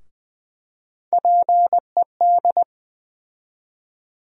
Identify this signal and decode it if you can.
Morse code, heard as a keyed tone, about 20 wpm, 700 Hz. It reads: PED
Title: USA Call Sign Postfixes - 3 Letters 20wpm